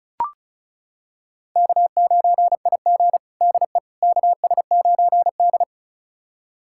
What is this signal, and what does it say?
Morse code, heard as a keyed tone, about 36 wpm, 700 Hz. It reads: K9IG DE KS9D
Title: USA Callsign DE Callsign - 4 Characters - 35wpm